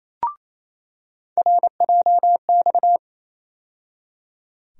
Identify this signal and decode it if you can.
Morse code, heard as a keyed tone, about 28 wpm, 700 Hz. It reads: RJX
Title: USA Call Sign Postfixes - 3 Letters 28wpm